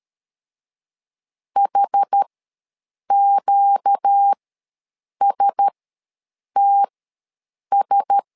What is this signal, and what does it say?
Morse code, heard as a keyed tone, about 13 wpm, spaced as for 7 wpm, 800 Hz. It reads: HQSTS
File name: LEZIONI / 4 Q1S ws 12wpm 1827